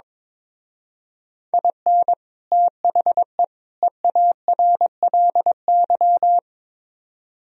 Morse code, heard as a keyed tone, about 23 wpm, 700 Hz. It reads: IN THE EARLY
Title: Speed-Racing - Sets of 3 Words 22wpm